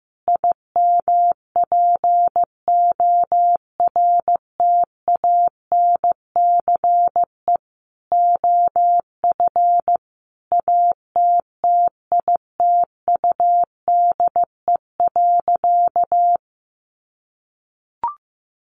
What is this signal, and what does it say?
Morse code, heard as a keyed tone, about 15 wpm, 700 Hz. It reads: IMPORTANCE OF ATTITUDE.
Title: Sentences from Top 1300 Words 15wpm